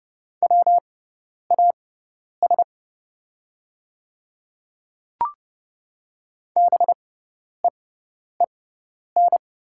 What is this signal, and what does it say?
Morse code, heard as a keyed tone, about 30 wpm, spaced as for 10 wpm, 700 Hz. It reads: WAS BEEN
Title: 6x Character Spacing - Top 300 Words - 30wpm